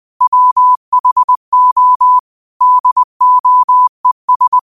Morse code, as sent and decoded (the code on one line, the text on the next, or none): .-- .... --- / -.. --- . ...
WHO DOES